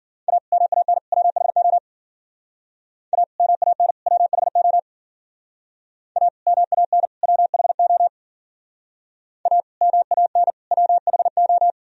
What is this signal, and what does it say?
Morse code, heard as a keyed tone, about 50 wpm, 700 Hz. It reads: A MAN WHO A MAN WHO A MAN WHO A MAN WHO